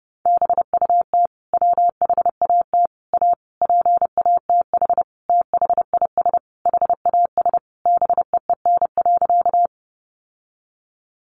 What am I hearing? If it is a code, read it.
Morse code, heard as a keyed tone, about 30 wpm, 700 Hz. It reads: BUT WHAT A PATH THIS HAS BEEN.
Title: Sentences from Top 1400 Words 30wpm